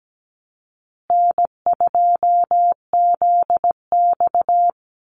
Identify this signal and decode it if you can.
Morse code, heard as a keyed tone, about 17 wpm, 700 Hz. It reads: N2ZX
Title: Easy Call Signs Repeated 2x - 17wpm